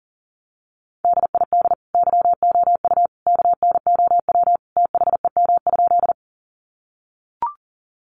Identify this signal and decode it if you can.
Morse code, heard as a keyed tone, about 40 wpm, 700 Hz. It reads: DID YOU KNOW THEM?
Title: Sentences from Top 100 Words 40wpm